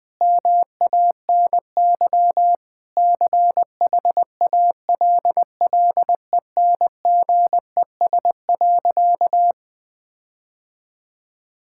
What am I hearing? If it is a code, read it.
Morse code, heard as a keyed tone, about 20 wpm, 700 Hz. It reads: MANY CHALLENGES.
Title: Sentences with 7-Word Phrases - 20wpm